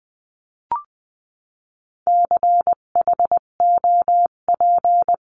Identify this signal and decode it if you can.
Morse code, heard as a keyed tone, about 20 wpm, 700 Hz. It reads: CHOP
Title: 258 Lesson 42c - Words 16-20wpm